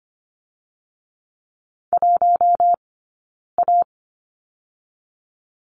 Morse code, heard as a keyed tone, about 25 wpm, 700 Hz. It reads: 1 A
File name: course / 25:10wpm / 092 Lesson 16d - Letter c - Two Characters 25wpm